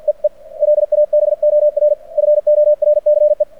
Morse code, heard as a keyed tone, about 49 wpm, 600 Hz. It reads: EE LABOR FORCE
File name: archived / CRSnet